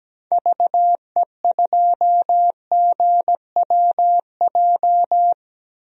Morse code, heard as a keyed tone, about 17 wpm, 700 Hz. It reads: VE2GWJ
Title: DXCC-100 Call Signs - 6 Characters - 17wpm